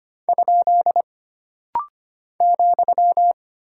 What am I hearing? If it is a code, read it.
Morse code, heard as a keyed tone, about 25 wpm, 700 Hz. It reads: ? ,
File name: Single Punctuation-Prosign - Rapid-Fire - 25wpm